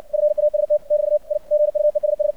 Morse code, heard as a keyed tone, about 64 wpm, 600 Hz. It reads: FAST <BT> I GUESS
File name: archived / CRSnet